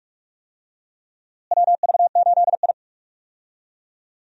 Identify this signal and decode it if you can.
Morse code, heard as a keyed tone, about 46 wpm, 700 Hz. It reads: WV8I
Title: USA Call Signs - 4 Characters - Repeated 3x - 45wpm